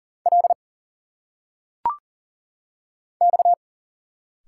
Morse code, heard as a keyed tone, about 39 wpm, 700 Hz. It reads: L X
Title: Single Letters - Phonetic Pronunciation 40wpm